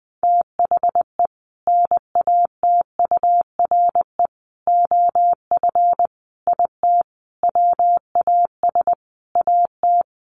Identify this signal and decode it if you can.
Morse code, heard as a keyed tone, about 20 wpm, 700 Hz. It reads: THE NATURE OF IT WAS AT
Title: Sentences from Top 1400 Words 20wpm